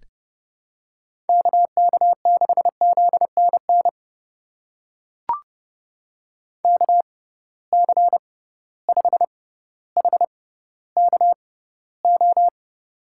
Morse code, heard as a keyed tone, about 30 wpm, spaced as for 15 wpm, 700 Hz. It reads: KK6ZNN KC5HKO
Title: USA Call Signs - 6 Characters - 6x Character Spacing - 30wpm